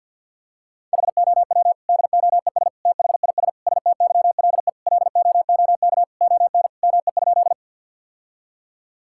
Morse code, heard as a keyed tone, about 50 wpm, 700 Hz. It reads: HOW DOES THIS STYLE LOOK ON ME?
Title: Sentences from Top 1000 Words - Review Part 5 - 50wpm